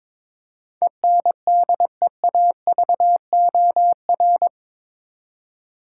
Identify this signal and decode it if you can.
Morse code, heard as a keyed tone, about 22 wpm, 700 Hz. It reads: ENDEAVOR
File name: practice / 22wpm / Eight Letter Words - 22wpm